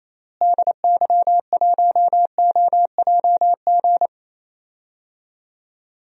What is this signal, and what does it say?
Morse code, heard as a keyed tone, about 28 wpm, 700 Hz. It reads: DY1OJG
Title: DXCC-100 Call Signs - 6 Characters - Repeated 2x - 28wpm